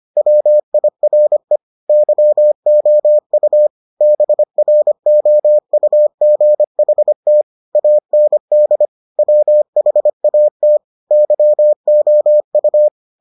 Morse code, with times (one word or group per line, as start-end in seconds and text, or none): 0.2-1.6 WIRE
1.9-3.7 YOU
4.0-7.4 BROUGHT
7.8-8.9 AND
9.2-10.8 WHAT
11.1-12.9 YOU